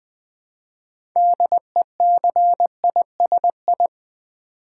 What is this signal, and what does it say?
Morse code, heard as a keyed tone, about 20 wpm, 700 Hz. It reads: DECISI
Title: Build A Word - 2 Letters At A Time - 8-Letter Words - 20wpm